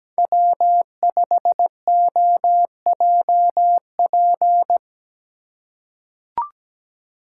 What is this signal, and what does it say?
Morse code, heard as a keyed tone, about 17 wpm, 700 Hz. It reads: W5OJP
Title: USA Call Signs - 5 Characters - 4x Character Spacing - 17wpm